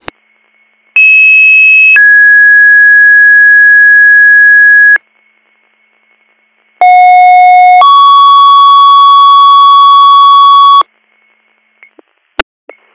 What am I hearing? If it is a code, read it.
Two-tone page, heard as a one-second tone, then a three-second tone, about 2600 then 1650 Hz, about 730 then 1120 Hz. It is heard from 1.0 to 10.8 s.